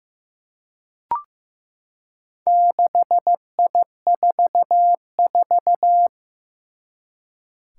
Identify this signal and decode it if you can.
Morse code, heard as a keyed tone, about 15 wpm, 700 Hz. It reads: 6I44
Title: Four Challenging Letter-Number Combinations 15wpm